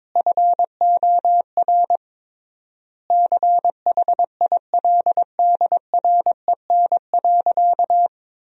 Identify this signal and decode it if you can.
Morse code, heard as a keyed tone, about 22 wpm, 700 Hz. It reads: FOR CHILDREN.
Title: 3x Word-Spacing - Sentences from Top 400 Words - 22wpm